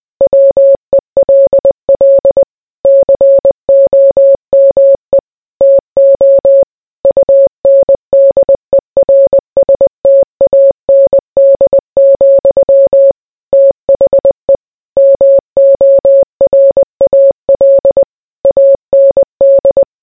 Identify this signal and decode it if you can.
Morse code, heard as a keyed tone, about 20 wpm, 550 Hz. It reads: WELL COME TO UNDERSTAND, THE MORAL AND